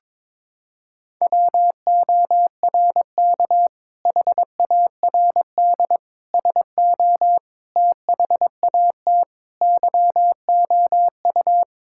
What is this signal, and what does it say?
Morse code, heard as a keyed tone, about 22 wpm, 700 Hz. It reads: WORK HARD SO THAT YOU
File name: practice / 22wpm / Sentences with 2-Word Phrases - 22wpm